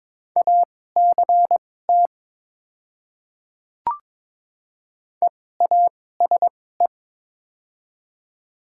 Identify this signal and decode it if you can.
Morse code, heard as a keyed tone, about 22 wpm, spaced as for 16 wpm, 700 Hz. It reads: ACT EASE
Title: 2x Character Spacing - Top 500 Words - 22wpm